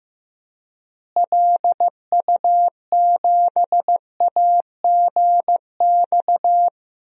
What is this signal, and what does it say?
Morse code, heard as a keyed tone, about 14 wpm, 700 Hz. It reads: LU7AGX
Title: DXCC-100 Call Signs - 6 Characters - Repeated 3x - 15wpm